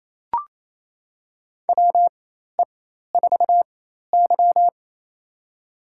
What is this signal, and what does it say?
Morse code, heard as a keyed tone, about 27 wpm, spaced as for 13 wpm, 700 Hz. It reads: WE4Y
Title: USA Call Signs - 4 Characters - 4x Character Spacing - 28wpm